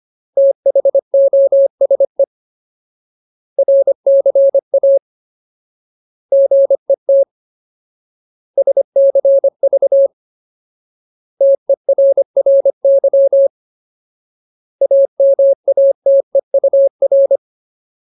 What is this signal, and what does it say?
Morse code, heard as a keyed tone, about 25 wpm, 550 Hz. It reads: THOSE RCA GET SCV TERRY AMATEUR
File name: Ultimate Word List / 25WPM Ultimate